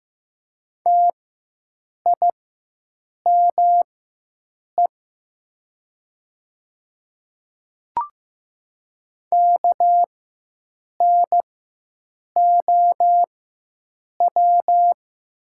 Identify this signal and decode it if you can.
Morse code, heard as a keyed tone, about 15 wpm, spaced as for 7 wpm, 700 Hz. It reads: TIME KNOW
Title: 4x Character Spacing - Top 100 Words - 15wpm